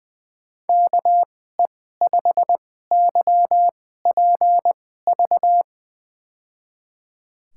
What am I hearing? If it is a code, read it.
Morse code, heard as a keyed tone, about 20 wpm, spaced as for 14 wpm, 700 Hz. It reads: KE5YPV
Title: USA Call Signs - 6 Characters - 2x Character Spacing - 20wpm